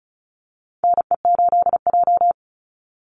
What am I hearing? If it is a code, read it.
Morse code, heard as a keyed tone, about 34 wpm, 700 Hz. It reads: NE8J
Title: Call Signs - 35wpm